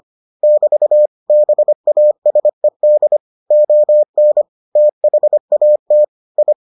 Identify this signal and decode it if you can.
Morse code, heard as a keyed tone, about 25 wpm, 600 Hz. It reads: <BT> BASED ON THAT I